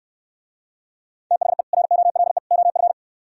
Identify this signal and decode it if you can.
Morse code, heard as a keyed tone, about 66 wpm, 700 Hz. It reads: THE ROLE OF